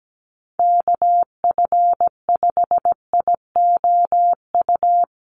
Morse code, heard as a keyed tone, about 17 wpm, 700 Hz. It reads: KF5IOU
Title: USA Call Signs Repeated 2x 17wpm